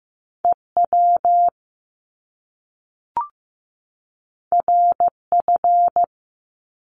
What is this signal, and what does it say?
Morse code, heard as a keyed tone, about 15 wpm, 700 Hz. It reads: EW RF
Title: Top Two Letter Combinations in Words 15wpm